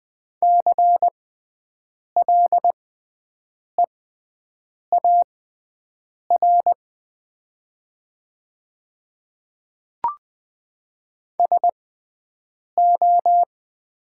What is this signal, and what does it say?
Morse code, heard as a keyed tone, about 20 wpm, spaced as for 7 wpm, 700 Hz. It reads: CLEAR SO
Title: 6x Character Spacing - Top 500 Words - 20wpm